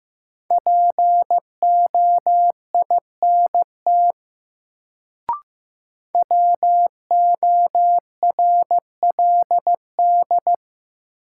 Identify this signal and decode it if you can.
Morse code, heard as a keyed tone, about 15 wpm, 700 Hz. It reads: POINT WORLD
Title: Sentences from Top 200 Words - New Word Drill - 15wpm